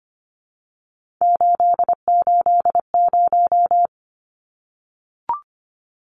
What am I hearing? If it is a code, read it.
Morse code, heard as a keyed tone, about 25 wpm, 700 Hz. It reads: 880
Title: Numbers - 3-Digit 25wpm